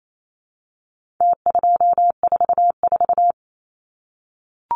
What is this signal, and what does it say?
Morse code, heard as a keyed tone, about 28 wpm, 700 Hz. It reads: T244
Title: Numbers - Contest Style - 4-Digit 28wpm